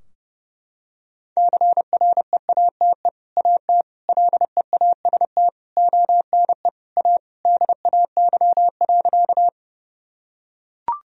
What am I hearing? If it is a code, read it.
Morse code, heard as a keyed tone, about 30 wpm, 700 Hz. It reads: CREATE AT LEAST ONE A DAY.
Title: Sentences from Top 1000 Words - Review Part 3 - 30wpm